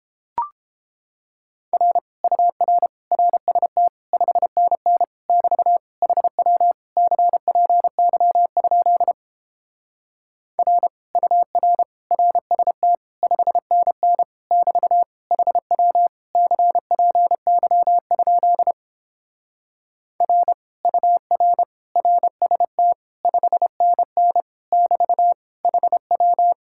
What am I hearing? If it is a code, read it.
Morse code, heard as a keyed tone, about 31 wpm, 700 Hz. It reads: R UR RST 5NN <BT> HW CPY? R UR RST 5NN <BT> HW CPY? R UR RST 5NN <BT> HW